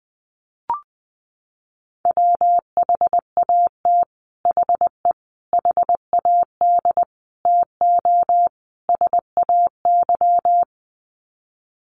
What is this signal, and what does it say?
Morse code, heard as a keyed tone, about 19 wpm, 700 Hz. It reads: WHAT HE HAD TO SAY
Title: Sets of 5 Words Encore2 20wpm